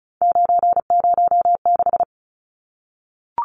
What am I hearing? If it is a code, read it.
Morse code, heard as a keyed tone, about 34 wpm, 700 Hz. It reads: 906
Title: Numbers - 3-Digit 35wpm